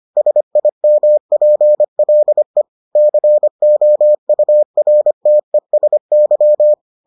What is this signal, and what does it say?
Morse code, heard as a keyed tone, about 25 wpm, 600 Hz. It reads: SIMPLE COURTESY